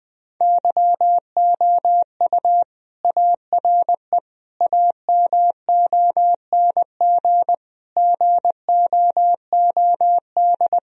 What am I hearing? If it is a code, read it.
Morse code, heard as a keyed tone, about 20 wpm, 700 Hz. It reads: YOU ARE AMONG GOOD
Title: Sentences from Top 1000 Words - Review Part 3 - 20wpm